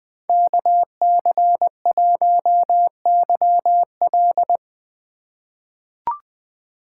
Morse code, heard as a keyed tone, about 20 wpm, 700 Hz. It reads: KC1YL
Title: 2023 POTA Activator Call Signs - 20wpm